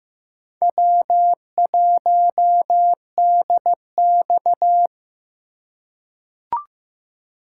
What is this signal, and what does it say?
Morse code, heard as a keyed tone, about 15 wpm, 700 Hz. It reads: W1DX
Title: K1USN SST QSO Practice - 15wpm